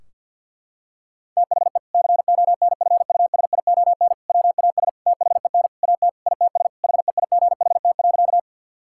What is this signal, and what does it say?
Morse code, heard as a keyed tone, about 50 wpm, 700 Hz. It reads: THE CONFUSION WAS THEN AT ITS HEIGHT.
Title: Sentences from Top 1600 Words 50wpm